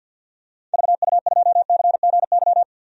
Morse code, heard as a keyed tone, about 51 wpm, 700 Hz. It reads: VU2CGY
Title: DXCC-100 Call Signs - 6 Characters - 50wpm